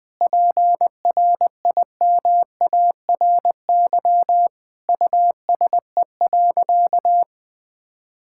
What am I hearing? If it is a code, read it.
Morse code, heard as a keyed tone, about 20 wpm, 700 Hz. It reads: PRIMARY USE.
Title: Sentences from Top 1000 Words - Review Part 5 - 20wpm